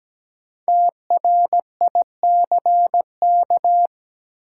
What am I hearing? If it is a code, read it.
Morse code, heard as a keyed tone, about 17 wpm, 700 Hz. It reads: TRICK
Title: Sentences from Top 1500 Words - New Word Drill Alt - 17wpm